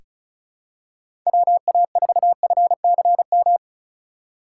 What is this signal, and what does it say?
Morse code, heard as a keyed tone, about 36 wpm, 700 Hz. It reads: WA4FCM
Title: USA Call Signs - 2x Character Spacing - 35wpm